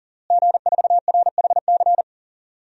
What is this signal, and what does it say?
Morse code, heard as a keyed tone, about 40 wpm, 700 Hz. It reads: G4RHC
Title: DXCC-100 Call Signs - 5 Characters - Repeated 3x - 40wpm